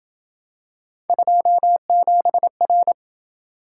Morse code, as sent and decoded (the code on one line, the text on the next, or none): ..--- --... .-.
27R